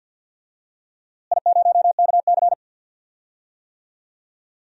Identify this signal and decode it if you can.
Morse code, heard as a keyed tone, about 50 wpm, 700 Hz. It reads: I0KC
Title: DXCC-100 Call Signs - 4 Characters - Repeated 2x - 50wpm